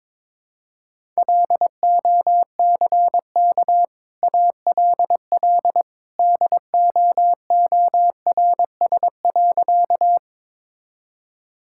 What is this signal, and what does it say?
Morse code, heard as a keyed tone, about 22 wpm, 690 Hz. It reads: LOCK ALL DOORS.
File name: practice / 22wpm / Easy Sentences with Hints - 22wpm